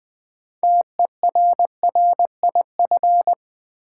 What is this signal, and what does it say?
Morse code, heard as a keyed tone, about 20 wpm, 700 Hz. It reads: TERRIF
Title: Build A Word - 2 Letters At A Time - 8-Letter Words - 20wpm